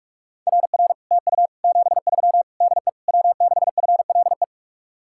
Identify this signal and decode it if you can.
Morse code, heard as a keyed tone, about 48 wpm, 700 Hz. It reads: RR TU 73 DE W6FLE